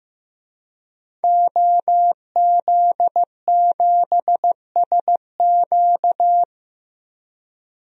Morse code, heard as a keyed tone, about 15 wpm, 700 Hz. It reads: OZ7SQ
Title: DXCC-100 Call Signs - 5 Characters - 15wpm